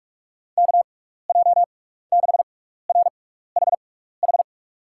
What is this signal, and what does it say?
Morse code, heard as a keyed tone, about 46 wpm, spaced as for 15 wpm, 700 Hz. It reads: KJ6RHH
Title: USA Call Signs - 6 Characters - 6x Character Spacing - 45wpm